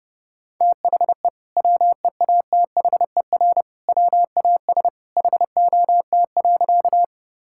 Morse code, heard as a keyed tone, about 30 wpm, 700 Hz. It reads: THE WEATHER WAS HOT.
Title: Sentences from Top 1000 Words - Review Part 5 - 30wpm